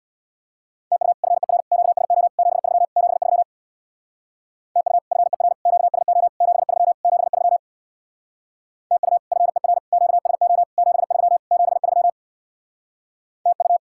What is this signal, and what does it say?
Morse code, heard as a keyed tone, about 65 wpm, 700 Hz. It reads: TU FER QSO 73 73 TU FER QSO 73 73 TU FER QSO 73 73 TU